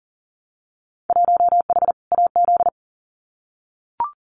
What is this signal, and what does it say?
Morse code, heard as a keyed tone, about 40 wpm, 700 Hz. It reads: 1H AZ
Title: Winter Field Day Exchanges 40wpm